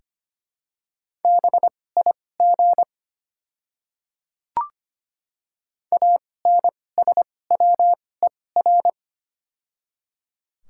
Morse code, heard as a keyed tone, about 25 wpm, spaced as for 18 wpm, 700 Hz. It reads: BIG ANSWER
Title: 2x Character Spacing - Top 300 Words - 25wpm